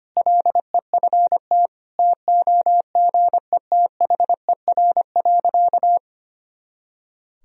Morse code, heard as a keyed tone, about 25 wpm, 700 Hz. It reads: LEFT TOGETHER.
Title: Sentences from Top 400 Words and Abbreviations 25wpm